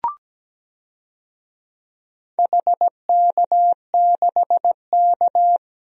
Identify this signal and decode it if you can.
Morse code, heard as a keyed tone, about 17 wpm, 700 Hz. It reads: HK6K
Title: DXCC-100 Call Signs - 4 Characters - Repeated 2x - 17wpm